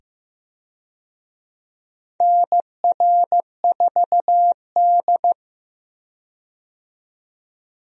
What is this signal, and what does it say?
Morse code, heard as a keyed tone, about 15 wpm, 700 Hz. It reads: NR4D
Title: USA Call Signs - 4 Characters - 15wpm